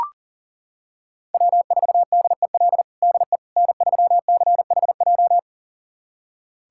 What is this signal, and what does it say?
Morse code, heard as a keyed tone, about 40 wpm, 700 Hz. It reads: W4DEL DE N3CHJ
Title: USA Callsign DE Callsign - 5 Characters - 40wpm